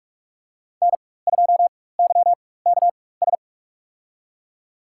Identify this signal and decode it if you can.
Morse code, heard as a keyed tone, about 43 wpm, spaced as for 21 wpm, 700 Hz. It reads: N2YKS
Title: USA Call Signs - 5 Characters - 4x Character Spacing - 45wpm